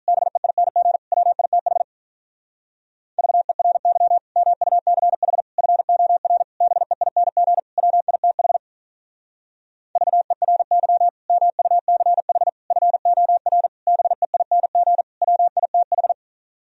Morse code, heard as a keyed tone, about 42 wpm, 700 Hz. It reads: BEING WITH VERY MUCH FOR BEING WITH VERY MUCH FOR BEING WITH
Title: Speed-Racing - Sets of 5 Words 35wpm